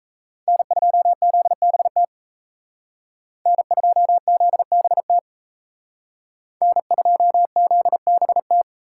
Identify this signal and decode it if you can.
Morse code, heard as a keyed tone, about 36 wpm, 700 Hz. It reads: N2ZBT N2ZBT N2ZBT